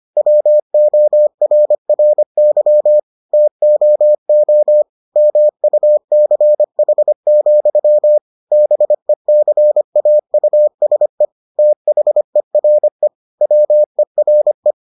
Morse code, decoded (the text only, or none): WORRY TOO MUCH, BECAUSE THERE WERE